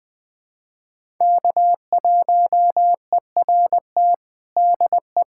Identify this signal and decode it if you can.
Morse code, heard as a keyed tone, about 20 wpm, 700 Hz. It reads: K1ERT DE
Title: USA Callsign DE Callsign - 5 Characters - 20wpm